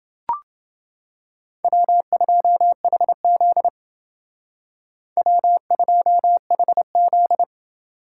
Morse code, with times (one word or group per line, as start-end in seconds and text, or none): 1.6-3.7 W2HZ
5.2-7.4 W2HZ